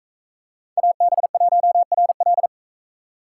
Morse code, heard as a keyed tone, about 44 wpm, 700 Hz. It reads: AB1RL